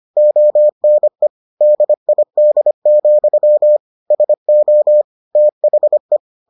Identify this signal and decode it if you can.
Morse code, heard as a keyed tone, about 25 wpm, 600 Hz. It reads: ONE DID, SO THE